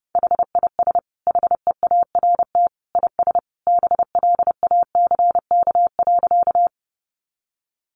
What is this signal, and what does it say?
Morse code, heard as a keyed tone, about 30 wpm, 700 Hz. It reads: HIS HEART IS BLACK.
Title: Sentences from Top 1200 Words 30wpm